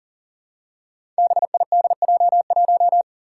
Morse code, heard as a keyed tone, about 38 wpm, 700 Hz. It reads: BIDJ1
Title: Five Challenging Letter-Number Combinations 40wpm